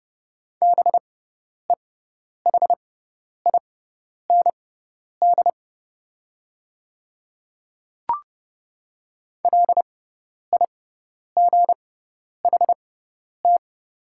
Morse code, heard as a keyed tone, about 30 wpm, spaced as for 10 wpm, 700 Hz. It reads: BEHIND LIGHT
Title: 6x Character Spacing - Top 500 Words - 30wpm